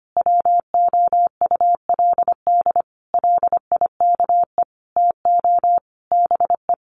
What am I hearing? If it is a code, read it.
Morse code, heard as a keyed tone, about 25 wpm, 700 Hz. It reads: WOULD LIKE TO BE
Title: Sentences from Top 1500 Words 25wpm